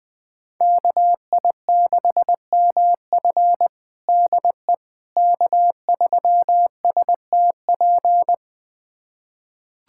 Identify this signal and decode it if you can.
Morse code, heard as a keyed tone, about 20 wpm, 700 Hz. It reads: KI6MF DE K3STP
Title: USA Callsign DE Callsign - 5 Characters - 20wpm